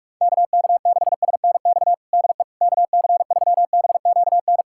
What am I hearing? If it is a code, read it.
Morse code, heard as a keyed tone, about 42 wpm, 700 Hz. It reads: KK6SNX DE KC3BQN
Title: USA Callsign DE Callsign - 6 Characters - 45wpm